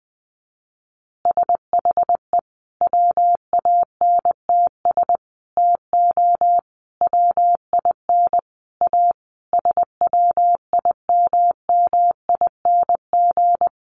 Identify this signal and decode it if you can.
Morse code, heard as a keyed tone, about 20 wpm, 700 Hz. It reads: SHE WANTS TO WIN A SWIMMING
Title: Additional Easy Phrases 20wpm